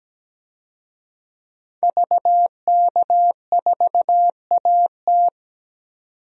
Morse code, heard as a keyed tone, about 17 wpm, 700 Hz. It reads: VK4AT
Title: DXCC-100 Call Signs - 5 Characters - 17wpm